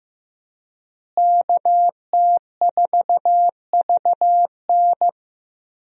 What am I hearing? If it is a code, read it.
Morse code, heard as a keyed tone, about 16 wpm, 700 Hz. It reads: KT4VN
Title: USA Call Signs - 5 Characters - Repeated 2x - 15wpm